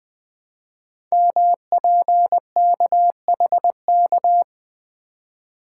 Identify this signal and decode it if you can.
Morse code, heard as a keyed tone, about 20 wpm, 700 Hz. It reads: MPKHK
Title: Code Groups - 5 Letter:Numbers - 20wpm